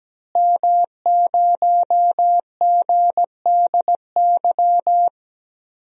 Morse code, heard as a keyed tone, about 17 wpm, 700 Hz. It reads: M0GDY
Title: DXCC-100 Call Signs - 5 Characters - Repeated 3x - 17wpm